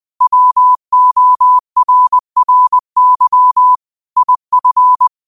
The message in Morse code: WORRY IF

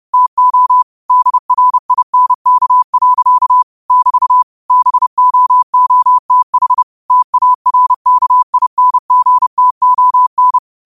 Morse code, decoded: TO DRINK. <BT> BOOTH TARKINGTON

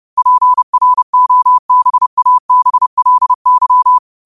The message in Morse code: PROBABLY

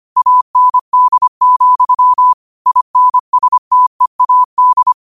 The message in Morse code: AND, INSTEAD

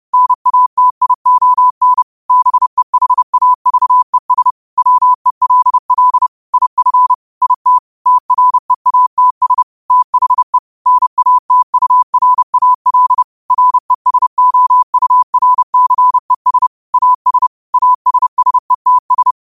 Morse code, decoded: NATION BEHAVES WELL IF IT TREATS THE NATURAL RESOURCES AS ASSETS

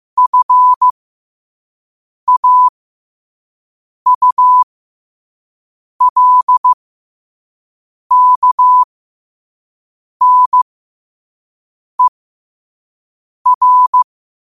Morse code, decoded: FAULKNER

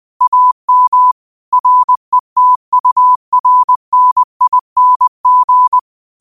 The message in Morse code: AM RETURNING

